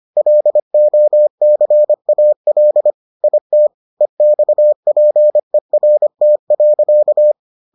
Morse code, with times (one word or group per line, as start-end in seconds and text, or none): 0.2-2.9 LOCAL
3.2-3.7 IT
4.0-7.3 EXPERT.